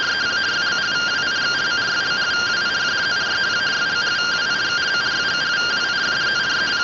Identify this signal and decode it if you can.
Two-tone teleprinter signal, about 170 Hz shift.